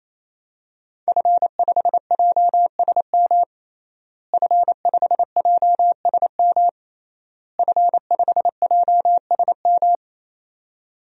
Morse code, heard as a keyed tone, about 28 wpm, 700 Hz. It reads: F5JSM F5JSM F5JSM
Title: DXCC-100 Call Signs - 5 Characters - Repeated 3x - 28wpm